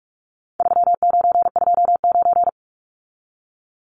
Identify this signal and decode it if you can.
Morse code, heard as a keyed tone, about 44 wpm, 700 Hz. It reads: 3929
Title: Numbers - 4-Digit 45wpm